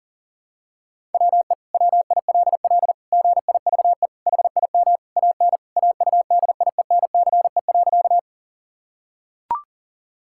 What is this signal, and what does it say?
Morse code, heard as a keyed tone, about 40 wpm, 700 Hz. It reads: WE WILL GIVE HIM AN AUDIENCE.
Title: Sentences from Top 1000 Words - Review Part 5 - 40wpm